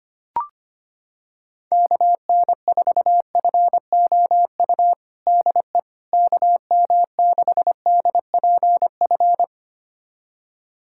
Morse code, heard as a keyed tone, about 25 wpm, 700 Hz. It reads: KN4FOU DE KM6DPF
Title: USA Callsign DE Callsign - 6 Characters - 25wpm